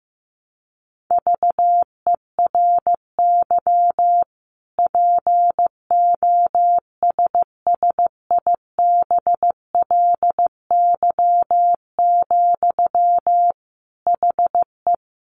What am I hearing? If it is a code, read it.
Morse code, heard as a keyed tone, about 15 wpm, 700 Hz. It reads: VERY POSSIBLY, HE
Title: Sentences from Top 1300 Words 15wpm